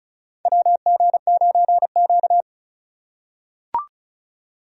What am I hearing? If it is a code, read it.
Morse code, heard as a keyed tone, about 36 wpm, 700 Hz. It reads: WG9Q
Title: USA Call Signs - 4 Characters - 3x Character Spacing - 35wpm